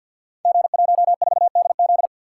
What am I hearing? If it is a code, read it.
Morse code, heard as a keyed tone, about 49 wpm, 700 Hz. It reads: G14DZ
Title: Code Groups - 5 Letter:Numbers - 50wpm